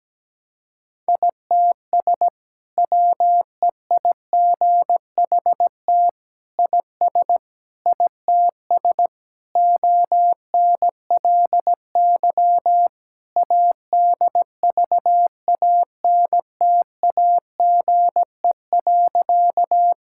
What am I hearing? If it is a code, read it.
Morse code, heard as a keyed tone, about 17 wpm, 700 Hz. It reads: ITS WEIGHT IS ITS ONLY ADVANTAGE.
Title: Sentences from Top 1000 Words - Review Part 6 - 17wpm